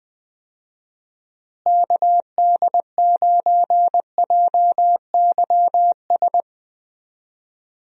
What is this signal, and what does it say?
Morse code, heard as a keyed tone, about 21 wpm, 700 Hz. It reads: KD9JYS
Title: USA Call Signs - 6 Characters - Repeated 3x - 20wpm